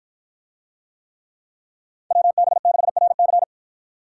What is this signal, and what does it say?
Morse code, heard as a keyed tone, about 53 wpm, 700 Hz. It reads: WB6RC